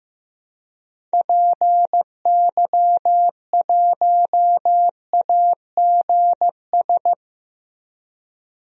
Morse code, heard as a keyed tone, about 16 wpm, 700 Hz. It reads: PY1AGS
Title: DXCC-100 Call Signs - 6 Characters - Repeated 2x - 15wpm